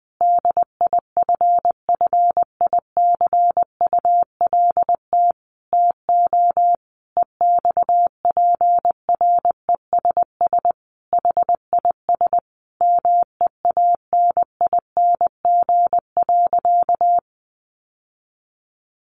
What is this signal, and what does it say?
Morse code, heard as a keyed tone, about 19 wpm, 700 Hz. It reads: DIFFICULT TO EXPRESS HIS MEANING.
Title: Sentences from Top 1100 Words 20wpm